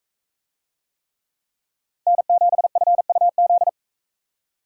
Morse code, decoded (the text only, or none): N7FUZ